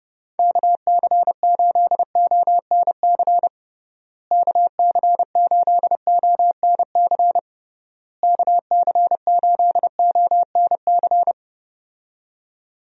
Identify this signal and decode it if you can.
Morse code, heard as a keyed tone, about 30 wpm, 700 Hz. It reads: KC8ONC KC8ONC KC8ONC
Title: USA Call Signs - 6 Characters - Repeated 3x - 30wpm